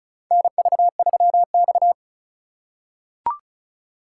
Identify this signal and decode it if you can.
Morse code, heard as a keyed tone, about 36 wpm, 700 Hz. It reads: NV3X
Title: USA Call Signs - 4 Characters - 2x Character Spacing - 35wpm